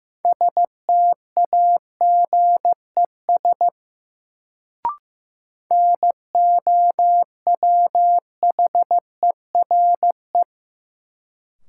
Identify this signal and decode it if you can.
Morse code, heard as a keyed tone, about 15 wpm, 700 Hz. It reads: STAGES NOWHERE
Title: Sets of 5 Words Encore - New Word Drill - 15wpm